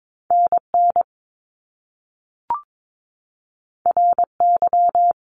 Morse code, heard as a keyed tone, about 22 wpm, 700 Hz. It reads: NN RY
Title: Top Two Letter Combinations in Words 22wpm